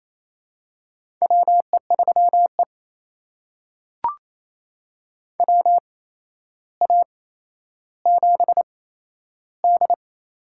Morse code, heard as a keyed tone, about 28 wpm, spaced as for 12 wpm, 700 Hz. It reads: WE3E WA7D
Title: USA Call Signs - 4 Characters - 8x Character Spacing - 28wpm